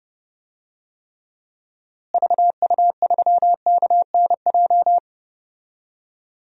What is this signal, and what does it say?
Morse code, heard as a keyed tone, about 29 wpm, 700 Hz. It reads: VU3KNJ
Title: DXCC-100 Call Signs - 6 Characters - Repeated 2x - 30wpm